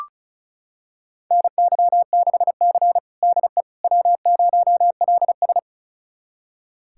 Morse code, heard as a keyed tone, about 34 wpm, 700 Hz. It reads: NY6C DE W0LS
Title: USA Callsign DE Callsign - 4 Characters - 35wpm